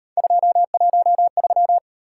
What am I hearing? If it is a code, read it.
Morse code, heard as a keyed tone, about 38 wpm, 700 Hz. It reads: 213